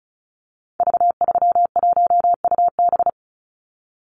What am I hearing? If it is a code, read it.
Morse code, heard as a keyed tone, about 34 wpm, 700 Hz. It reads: V31UB
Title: Call Signs - 35wpm